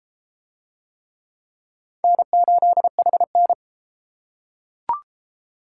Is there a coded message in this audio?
Yes: Morse code from 2.0 to 3.5 s, heard as a keyed tone, about 31 wpm, 700 Hz. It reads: N8HN